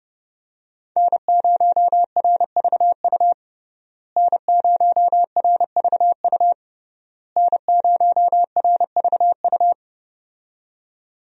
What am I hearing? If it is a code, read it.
Morse code, heard as a keyed tone, about 32 wpm, 700 Hz. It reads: N0RVU N0RVU N0RVU
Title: USA Call Signs - 5 Characters - Repeated 3x - 30wpm